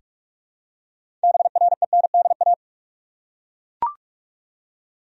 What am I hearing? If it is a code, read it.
Morse code, heard as a keyed tone, about 41 wpm, 700 Hz. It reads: BRENDA
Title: USA Common Names - 45wpm